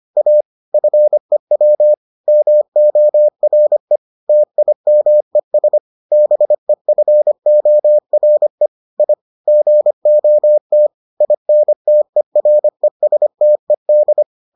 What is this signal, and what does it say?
Morse code, heard as a keyed tone, about 25 wpm, 600 Hz. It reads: A FEW MORE TIMES BEFORE I GOT INTERESTED